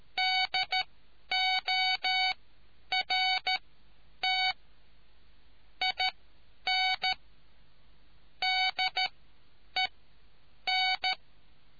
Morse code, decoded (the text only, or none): DORT IN DEN